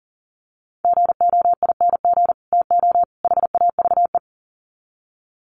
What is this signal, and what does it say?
Morse code, heard as a keyed tone, about 40 wpm, 700 Hz. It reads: GOING TO HAVE
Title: Sets of 3 Words 40wpm